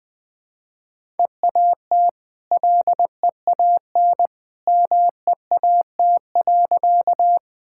Morse code, heard as a keyed tone, about 20 wpm, 700 Hz. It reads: EAT LEAN MEAT.
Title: Easy Sentences with Hints - 20wpm